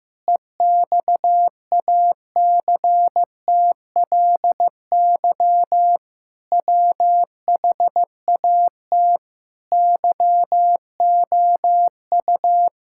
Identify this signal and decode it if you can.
Morse code, heard as a keyed tone, about 15 wpm, 700 Hz. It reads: EXACTLY WHAT YOU
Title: Sentences from Top 1000 Words - Review Part 6 - 15wpm